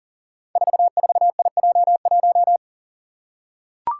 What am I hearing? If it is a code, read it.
Morse code, heard as a keyed tone, about 38 wpm, 690 Hz. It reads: 44IJ1
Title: Five Challenging Letter-Number Combinations 40wpm